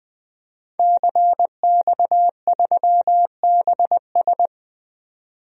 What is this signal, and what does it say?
Morse code, heard as a keyed tone, about 20 wpm, 700 Hz. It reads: CX3BS
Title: DXCC-100 Call Signs - 5 Characters - 20wpm